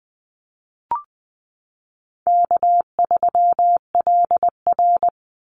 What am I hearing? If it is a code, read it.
Morse code, heard as a keyed tone, about 20 wpm, 700 Hz. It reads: K3LR
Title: Easy Call Signs - 20wpm